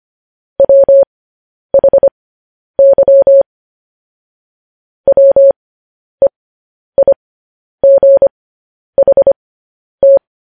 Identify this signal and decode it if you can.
Morse code, heard as a keyed tone, about 25 wpm, spaced as for 10 wpm, 550 Hz. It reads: WHY WEIGHT
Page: News Headlines in Morse Code at 10 WPM